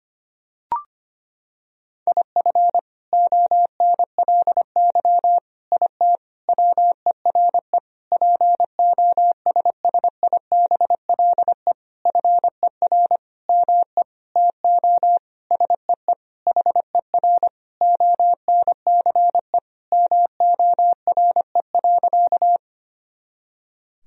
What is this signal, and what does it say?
Morse code, heard as a keyed tone, about 25 wpm, 700 Hz. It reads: IF ONLY IT WERE POSSIBLE FER ME TO SEE HER ONCE MORE.
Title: Sentences from Top 500 Words and Abbreviations 25wpm